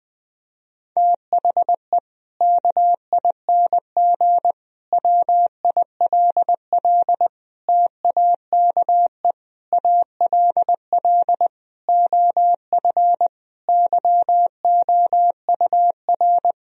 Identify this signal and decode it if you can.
Morse code, heard as a keyed tone, about 20 wpm, 700 Hz. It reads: THE KING WILL TAKE ALL OF YOUR
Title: Sentences from Top 1000 Words - Review Part 4 - 20wpm